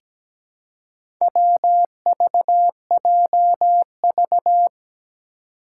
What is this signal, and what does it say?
Morse code, heard as a keyed tone, about 17 wpm, 700 Hz. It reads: WVJV
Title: Four Challenging Letter-Number Combinations 17wpm